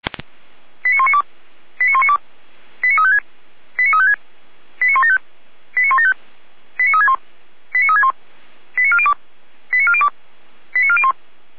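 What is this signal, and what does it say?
Five-tone selective call (ZVEI), heard as quick runs of five tones, tones of about 80 ms. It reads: 89192 89192 89356 89356 89175 89175 89361 89361 89402 89402 89401
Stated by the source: ZVEI five-tone selective call received by SDR, from feuerwehr-orlamuende.org fire brigade audio archive.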